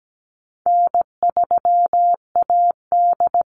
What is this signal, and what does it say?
Morse code, heard as a keyed tone, about 17 wpm, 700 Hz. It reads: N3AD
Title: Easy Call Signs - 17wpm